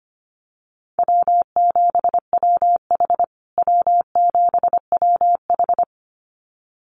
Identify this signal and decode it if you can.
Morse code, heard as a keyed tone, about 25 wpm, 700 Hz. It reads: W7WH W7WH
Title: USA Call Signs Repeated 2x 25wpm